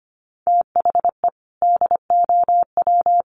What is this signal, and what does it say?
Morse code, heard as a keyed tone, about 25 wpm, 700 Hz. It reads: THE DOW
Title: Sets of 5 Words 25wpm